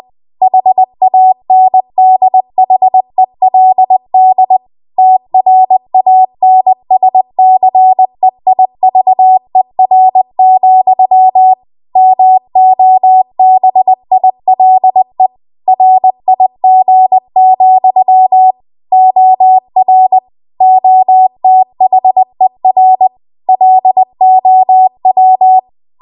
Morse code, heard as a keyed tone, about 20 wpm, 750 Hz. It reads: HANDHELD TRANSCEIVER, MOBILE RIG, OR OTHER LOW
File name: Archive / 20 WPM / 20WPM